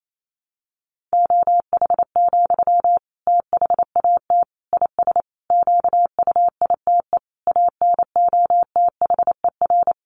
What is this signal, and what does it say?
Morse code, heard as a keyed tone, about 28 wpm, 700 Hz. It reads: OH, THAT IS QUITE ANOTHER
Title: Sentences from Top 700 Words 28wpm